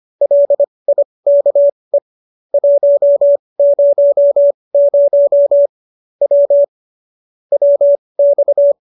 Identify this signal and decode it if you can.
Morse code, heard as a keyed tone, about 25 wpm, spaced as for 20 wpm, 570 Hz. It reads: LIKE 100 W WX